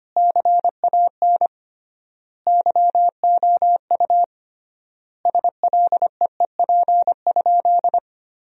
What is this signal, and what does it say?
Morse code, heard as a keyed tone, about 25 wpm, 700 Hz. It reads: CAN YOU SLEEP?
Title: 3x Word-Spacing - Sentences from Top 500 Words - 25wpm